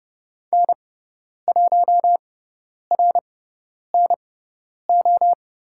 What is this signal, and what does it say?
Morse code, heard as a keyed tone, about 30 wpm, spaced as for 10 wpm, 700 Hz. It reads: N1RNO
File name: Lesson 17g - Letter u - Call Signs 30wpm